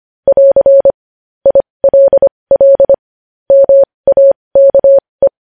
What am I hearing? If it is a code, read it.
Morse code, heard as a keyed tone, about 25 wpm, spaced as for 20 wpm, 550 Hz. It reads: <AR> ILL MAKE
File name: podcast / News headlines-1x-25x20